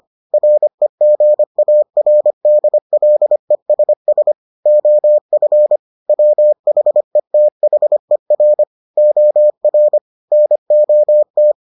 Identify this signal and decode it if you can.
Morse code, heard as a keyed tone, about 25 wpm, 600 Hz. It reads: REGARDLESS OF WHETHER OR NOT